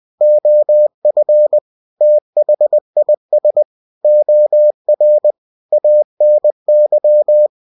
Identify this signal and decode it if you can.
Morse code, heard as a keyed tone, about 20 wpm, 600 Hz. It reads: OF THIS OR ANY
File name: cw-books / winnie-the-pooh / winnie-20wpm-page-0120